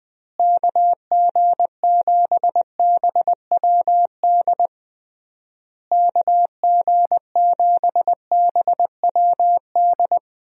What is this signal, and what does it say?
Morse code, heard as a keyed tone, about 20 wpm, 700 Hz. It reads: KG7BWD KG7BWD
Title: USA Call Signs - 6 Characters - Repeated 3x - 20wpm